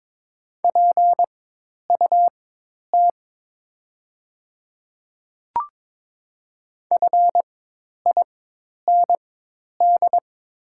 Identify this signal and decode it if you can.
Morse code, heard as a keyed tone, about 22 wpm, spaced as for 10 wpm, 700 Hz. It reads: PUT FIND
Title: 4x Character Spacing - Top 300 Words - 22wpm